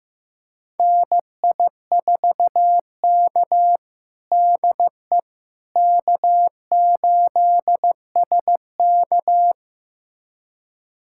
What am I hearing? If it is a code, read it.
Morse code, heard as a keyed tone, about 15 wpm, 700 Hz. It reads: NI4K DE K8SK
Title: USA Callsign DE Callsign - 4 Characters - 15wpm